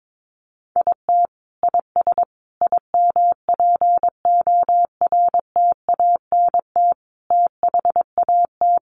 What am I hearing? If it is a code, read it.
Morse code, heard as a keyed tone, about 22 wpm, 700 Hz. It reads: IT IS IMPORTANT THAT
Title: Sentences from Top 600 Words 22wpm